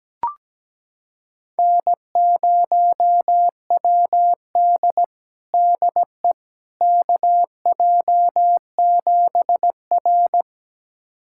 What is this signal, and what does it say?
Morse code, heard as a keyed tone, about 17 wpm, 700 Hz. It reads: N0WD DE KJ7R
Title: USA Callsign DE Callsign - 4 Characters - 17wpm